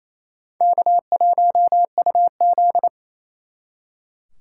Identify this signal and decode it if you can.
Morse code, heard as a keyed tone, about 36 wpm, 700 Hz. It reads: K1UZ